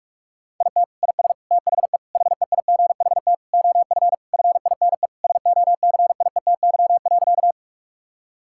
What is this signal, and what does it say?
Morse code, heard as a keyed tone, about 46 wpm, 700 Hz. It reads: IT IS THE HEIGHT OF FINE SOCIETY.
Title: Sentences from Top 1600 Words 45wpm